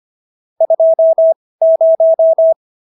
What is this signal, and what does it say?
Morse code, heard as a keyed tone, about 25 wpm, spaced as for 18 wpm, 650 Hz. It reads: 20